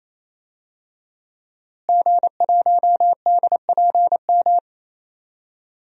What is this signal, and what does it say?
Morse code, heard as a keyed tone, about 28 wpm, 700 Hz. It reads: G1DPM
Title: DXCC-100 Call Signs - 5 Characters - Repeated 3x - 28wpm